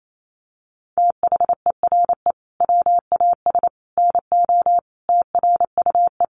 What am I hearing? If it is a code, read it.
Morse code, heard as a keyed tone, about 28 wpm, 700 Hz. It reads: THERE WAS NO TRUE
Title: Sentences from Top 1300 Words 28wpm